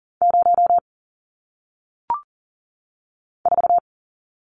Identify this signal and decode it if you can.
Morse code, heard as a keyed tone, about 39 wpm, 700 Hz. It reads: 0 4
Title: Numbers - 1-Digit 40wpm